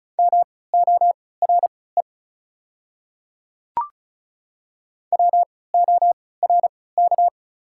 Morse code, heard as a keyed tone, about 36 wpm, spaced as for 20 wpm, 690 Hz. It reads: MORE WORK
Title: 3x Character Spacing - Top 200 Words - 35wpm